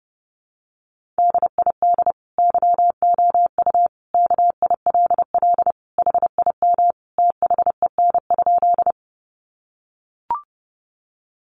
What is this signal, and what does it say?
Morse code, heard as a keyed tone, about 30 wpm, 700 Hz. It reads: DID YOU KILL HIM THEN?
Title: Sentences from Top 1400 Words 30wpm